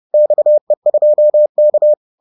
Morse code, heard as a keyed tone, about 30 wpm, 600 Hz. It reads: XE2K
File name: 30wpm-4